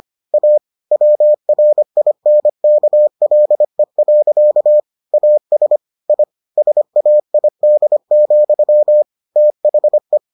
Morse code, heard as a keyed tone, about 25 wpm, 600 Hz. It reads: A WRINKLE. AS I SAID, THE